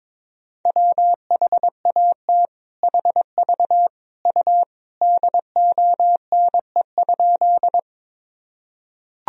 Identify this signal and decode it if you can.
Morse code, heard as a keyed tone, about 22 wpm, 700 Hz. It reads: WHAT HV U DONE?
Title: Sentences from Top 500 Words and Abbreviations 22wpm